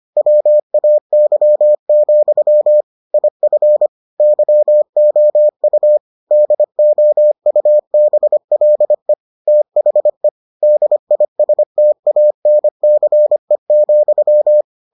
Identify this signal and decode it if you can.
Morse code, heard as a keyed tone, about 25 wpm, 600 Hz. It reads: WAY, IF YOU DOUBLE THE DISTANCE,